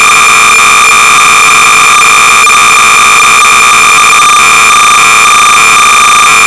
Cassette data (Kansas City type).
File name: sorcerer / 300baud / Grotnik wars-(with rapid fire option)(Ross Williams)-Sorcerer-300baud